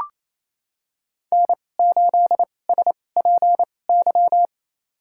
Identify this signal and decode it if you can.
Morse code, heard as a keyed tone, about 28 wpm, spaced as for 20 wpm, 700 Hz. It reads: N8SPY
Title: USA Call Signs - 5 Characters - 2x Character Spacing - 28wpm